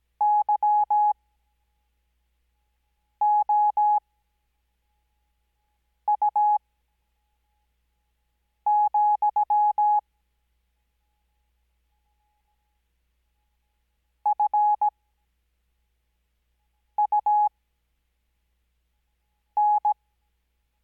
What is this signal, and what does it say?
Morse code, heard as a keyed tone, about 17 wpm, spaced as for 4 wpm, 840 Hz. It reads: YOU, FUN
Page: Farnsworth Spacing: 15WPM Character speed Spaced at 5WPM.